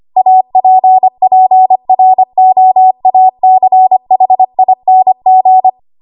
Morse code, heard as a keyed tone, about 25 wpm, 750 Hz. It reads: APPROACHING
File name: Archive / 25 WPM / 25WPM